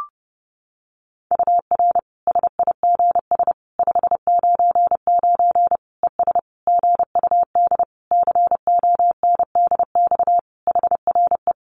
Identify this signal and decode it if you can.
Morse code, heard as a keyed tone, about 28 wpm, 700 Hz. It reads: UR SIGS 599 ES GUD CONDX HRE
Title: Common QSO Phrases 30wpm